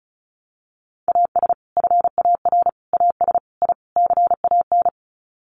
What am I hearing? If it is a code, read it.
Morse code, heard as a keyed tone, about 34 wpm, 700 Hz. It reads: AS FAR AS I CAN
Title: Sets of 5 Words Encore 35wpm